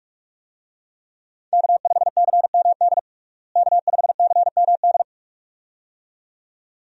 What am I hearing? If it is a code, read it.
Morse code, heard as a keyed tone, about 41 wpm, 700 Hz. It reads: K5CMD K5CMD
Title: USA Call Signs - 5 Characters - Repeated 2x - 45wpm